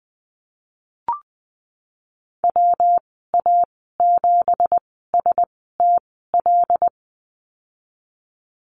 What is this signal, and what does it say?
Morse code, heard as a keyed tone, about 19 wpm, spaced as for 14 wpm, 700 Hz. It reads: WA7STL
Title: 179 Lesson 29g - Slash - Call Signs 20wpm